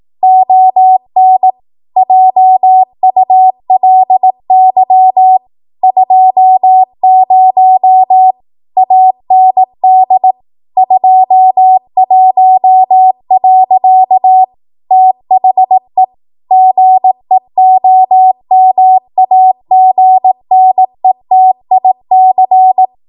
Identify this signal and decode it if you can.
Morse code, heard as a keyed tone, about 18 wpm, 750 Hz. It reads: ON JULY 20 AND 21. THE GEOMAGNETIC
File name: Morse / 18WPM